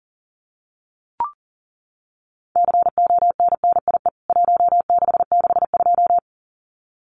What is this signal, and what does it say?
Morse code, heard as a keyed tone, about 37 wpm, 700 Hz. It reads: CONNIE 1662
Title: CWT Exchange 40wpm